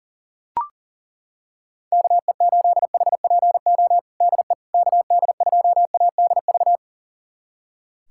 Morse code, heard as a keyed tone, about 40 wpm, 700 Hz. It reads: KE8HPO DE KD2ADV
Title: USA Callsign DE Callsign - 6 Characters - 40wpm